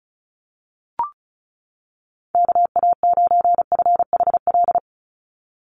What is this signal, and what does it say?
Morse code, heard as a keyed tone, about 34 wpm, 700 Hz. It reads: KA9FHL
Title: USA Call Signs 35wpm